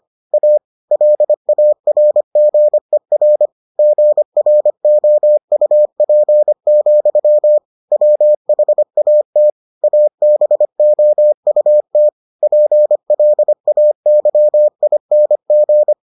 Morse code, heard as a keyed tone, about 25 wpm, 600 Hz. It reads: A LARGER GROUP, WHAT ABOUT PLAYING